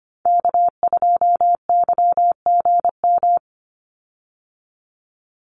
Morse code, heard as a keyed tone, about 25 wpm, 700 Hz. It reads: K2YGM
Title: Call Signs Repeated 2x - 25wpm